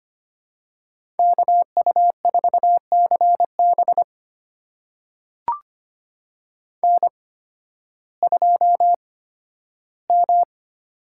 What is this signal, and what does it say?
Morse code, heard as a keyed tone, about 25 wpm, spaced as for 13 wpm, 700 Hz. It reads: KU4CB N2M
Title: USA Call Signs - 5 Characters - 8x Character Spacing - 25wpm